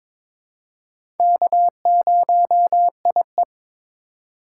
Morse code, heard as a keyed tone, about 21 wpm, 700 Hz. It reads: K0IE
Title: USA Call Signs - 2x Character Spacing - 22wpm